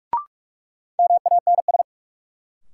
Morse code, heard as a keyed tone, about 42 wpm, 700 Hz. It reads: MANS
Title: Sentences from Top 1000 Words - Review Part 6 - New Word Drill - 45wpm